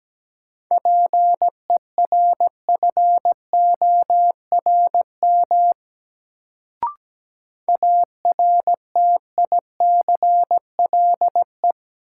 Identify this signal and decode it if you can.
Morse code, heard as a keyed tone, about 17 wpm, 700 Hz. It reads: PERFORM ARTICLE
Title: Sentences from Top 1000 Words - Review Part 5 - New Word Drill - 17wpm